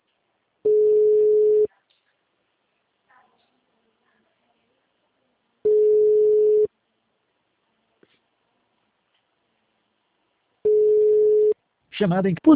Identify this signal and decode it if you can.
Ringback tone (Europe, 425 Hz), with speech.
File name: out- -2008- - - .88131